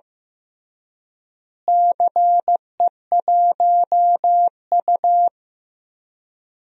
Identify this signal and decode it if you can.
Morse code, heard as a keyed tone, about 15 wpm, 700 Hz. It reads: CE1U
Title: DXCC-100 Call Signs - 4 Characters - Repeated 3x - 15wpm